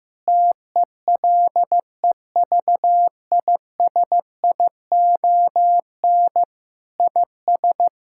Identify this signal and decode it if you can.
Morse code, heard as a keyed tone, about 15 wpm, 700 Hz. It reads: TELEVISION IS
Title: Sentences from Top 1000 Words - Review Part 6 - 15wpm